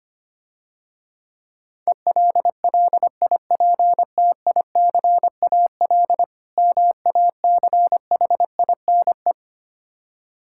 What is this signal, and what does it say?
Morse code, heard as a keyed tone, about 25 wpm, 700 Hz. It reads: ELLIPTICAL MACHINE
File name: practice / 25wpm / Difficult Words - Speed-Racing with Hints - 25wpm